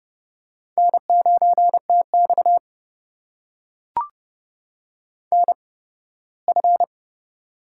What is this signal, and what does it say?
Morse code, heard as a keyed tone, about 30 wpm, spaced as for 18 wpm, 700 Hz. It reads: N9TX NF
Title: USA Call Signs - 4 Characters - 8x Character Spacing - 30wpm